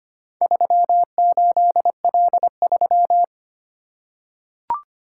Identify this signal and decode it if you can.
Morse code, heard as a keyed tone, about 25 wpm, 700 Hz. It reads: 38L3
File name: practice / 25wpm / Four Letter-Number Combinations - 25wpm